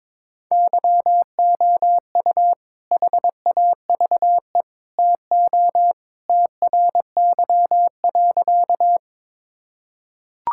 Morse code, heard as a keyed tone, about 22 wpm, 700 Hz. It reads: YOU HAVE TO TRY.
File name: practice / 22wpm / Sentences with 3-Word Phrases - 22wpm